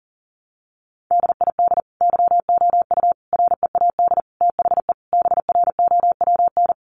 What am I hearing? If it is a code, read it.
Morse code, heard as a keyed tone, about 40 wpm, 700 Hz. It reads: DID YOU READ THE BROWN
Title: Sentences from Top 1200 Words 40wpm